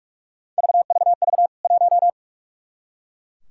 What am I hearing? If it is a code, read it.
Morse code, heard as a keyed tone, about 42 wpm, 700 Hz. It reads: VVV 1
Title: K1USN SST QSO Practice 45wpm